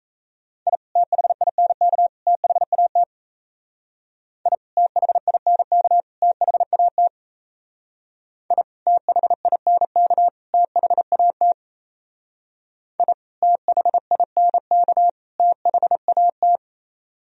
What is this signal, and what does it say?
Morse code, heard as a keyed tone, about 34 wpm, 700 Hz. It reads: I THINK THAT I THINK THAT I THINK THAT I THINK THAT
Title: Speed-Racing - Sets of 3 Words 28wpm